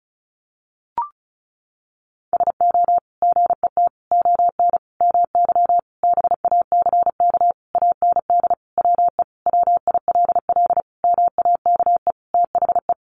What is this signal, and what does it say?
Morse code, heard as a keyed tone, about 34 wpm, 700 Hz. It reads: SO GET ON MY BACK AND WE WILL MAKE THE
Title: Sentences from Top 1000 Words 35wpm